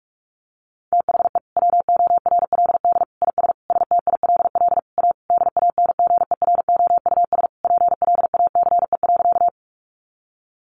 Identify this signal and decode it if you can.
Morse code, heard as a keyed tone, about 44 wpm, 700 Hz. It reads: THE WORLD IS STILL A DANGEROUS PLACE.
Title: Sentences from Top 1200 Words 45wpm